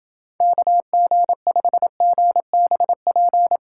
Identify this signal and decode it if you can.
Morse code, heard as a keyed tone, about 30 wpm, 700 Hz. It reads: KG5GBP